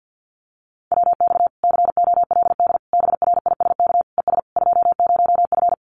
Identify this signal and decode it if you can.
Morse code, heard as a keyed tone, about 50 wpm, 700 Hz. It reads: WX COLD BRISK ES 20F
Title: Common QSO Phrases Encore 50wpm